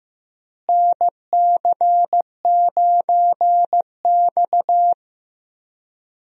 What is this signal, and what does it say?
Morse code, heard as a keyed tone, about 15 wpm, 700 Hz. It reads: NC9X
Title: USA Call Signs - 4 Characters - Repeated 3x - 15wpm